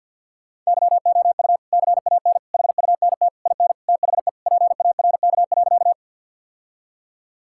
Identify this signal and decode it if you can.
Morse code, heard as a keyed tone, about 50 wpm, 700 Hz. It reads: YOU CAN HUNT IN THE PARK.
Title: Sentences from Top 1600 Words 50wpm